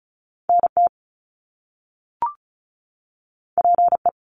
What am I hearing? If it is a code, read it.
Morse code, heard as a keyed tone, about 36 wpm, 700 Hz. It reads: NT PE
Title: Canadian Provinces and Territories 35wpm